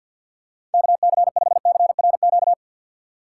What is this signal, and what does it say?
Morse code, heard as a keyed tone, about 53 wpm, 700 Hz. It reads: KC5CRQ
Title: USA Call Signs - 6 Characters - 4x Character Spacing - 50wpm